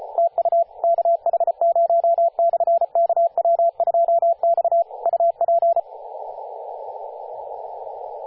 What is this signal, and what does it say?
Morse code, heard as a keyed tone, about 34 wpm, 680 Hz, with radio noise. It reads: TU KH0/KW2X UP
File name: posts / 2016-sep-22-0537z-kh0-kw2x-18-077mhz-cw